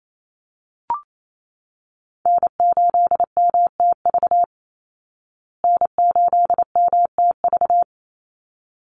Morse code, heard as a keyed tone, about 28 wpm, 700 Hz. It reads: N8MTV N8MTV
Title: Call Signs Repeated 3x - 28wpm